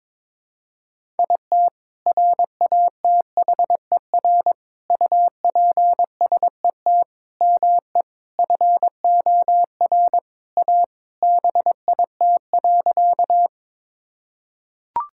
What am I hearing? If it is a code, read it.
Morse code, heard as a keyed tone, about 22 wpm, 700 Hz. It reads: IT RATHER UPSET ME FOR A BIT.
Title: Sentences from Top 1000 Words - Review Part 5 - 22wpm